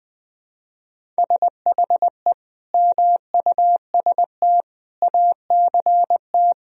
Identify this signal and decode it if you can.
Morse code, heard as a keyed tone, about 21 wpm, 700 Hz. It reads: SHE MUST ACT
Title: Sentences from Top 1000 Words - Review Part 1 - 20wpm